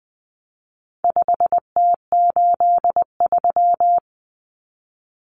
Morse code, heard as a keyed tone, about 20 wpm, 700 Hz. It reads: 5T83
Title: Common Cut Numbers - 4-Digit 20wpm